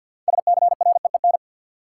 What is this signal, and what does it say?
Morse code, heard as a keyed tone, about 52 wpm, 700 Hz. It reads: SCREEN
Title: Sentences from Top 1000 Words - Review Part 6 - New Word Drill - 50wpm